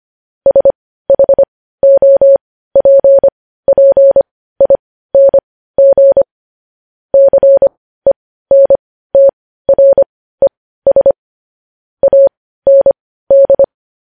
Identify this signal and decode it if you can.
Morse code, heard as a keyed tone, about 25 wpm, spaced as for 15 wpm, 550 Hz. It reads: SHOPPING CENTRES AND
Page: news-headlines-in-morse-code-at-15-wpm